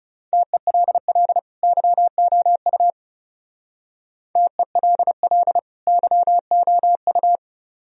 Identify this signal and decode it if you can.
Morse code, heard as a keyed tone, about 38 wpm, 700 Hz. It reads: TELL YOU TELL YOU